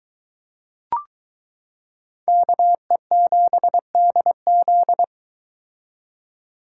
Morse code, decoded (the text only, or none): KE7DZ